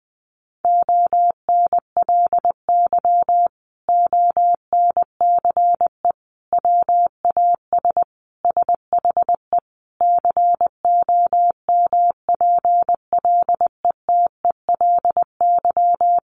Morse code, heard as a keyed tone, about 20 wpm, 700 Hz. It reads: ONLY ONCE WAS SHE COMPLETELY
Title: Sentences from Top 1100 Words 20wpm